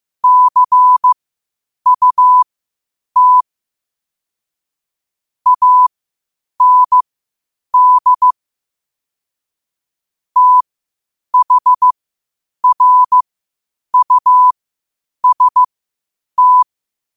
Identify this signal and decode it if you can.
Morse code, heard as a keyed tone, about 15 wpm, spaced as for 8 wpm, 1000 Hz. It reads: CUT AND THRUST
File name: qotdfiles / 8wpm